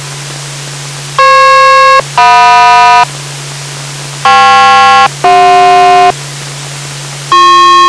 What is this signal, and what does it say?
Aircraft SELCAL, heard as two pairs of tones. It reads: FP-JM KP-CJ